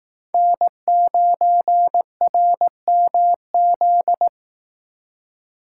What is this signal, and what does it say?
Morse code, heard as a keyed tone, about 18 wpm, 700 Hz. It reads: N9RMZ
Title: Speed Racing - USA 1x3 Call Signs 15wpm